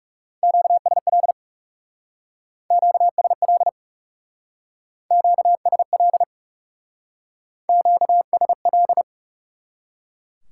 Morse code, heard as a keyed tone, about 38 wpm, 700 Hz. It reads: QSL QSL QSL QSL